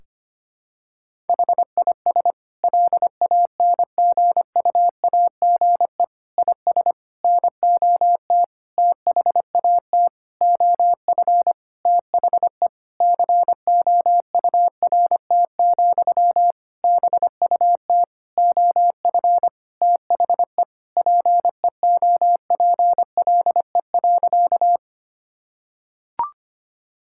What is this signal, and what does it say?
Morse code, heard as a keyed tone, about 25 wpm, 700 Hz. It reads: HIS LANGUAGE IS NOT THAT OF THE COURT, BUT OF THE PEOPLE.
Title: Sentences from Top 1600 Words 25wpm